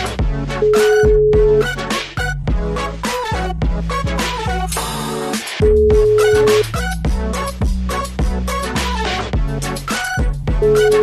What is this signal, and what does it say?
Ringback tone (Europe, 425 Hz), with noise.